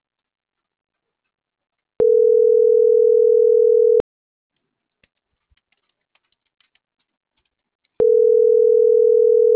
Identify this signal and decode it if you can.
Ringback tone (North America), 440 + 480 Hz.
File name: CENTRAL MAERSK / out adrian.moreno